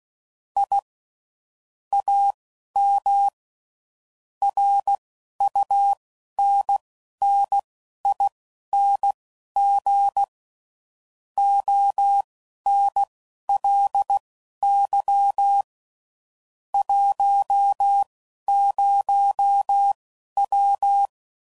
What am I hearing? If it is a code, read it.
Morse code, heard as a keyed tone, about 16 wpm, spaced as for 11 wpm, 800 Hz. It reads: I AM RUNNING ONLY 10W
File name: CW Code Groups @10WPM / QSOs / QSO kc7haa